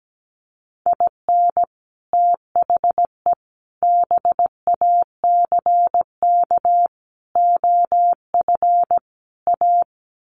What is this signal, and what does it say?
Morse code, heard as a keyed tone, about 17 wpm, 700 Hz. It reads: IN THE BACK OF A
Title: Sets of 5 Words Encore2 17wpm